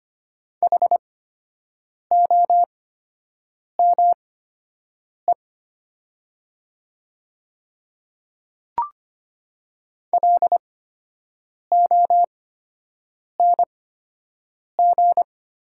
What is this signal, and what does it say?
Morse code, heard as a keyed tone, about 25 wpm, spaced as for 7 wpm, 700 Hz. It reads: HOME LONG
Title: 8x Character Spacing - Top 200 Words - 25wpm